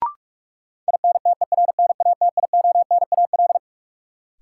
Morse code, heard as a keyed tone, about 42 wpm, 700 Hz. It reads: INTERNATIONAL
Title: Sentences from Top 1000 Words - Review Part 4 - New Word Drill - 45wpm